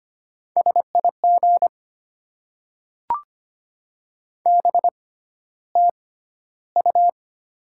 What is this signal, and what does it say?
Morse code, heard as a keyed tone, about 25 wpm, spaced as for 13 wpm, 690 Hz. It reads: SIG BTU
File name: Top 90 QSO Elements - 6x Character Spacing - 25wpm